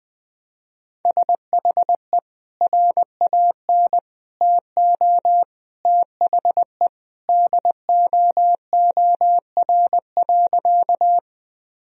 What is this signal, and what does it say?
Morse code, heard as a keyed tone, about 20 wpm, 700 Hz. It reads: SHE RAN TO THE DOOR.
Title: Sentences from Top 1000 Words - Review Part 3 - 20wpm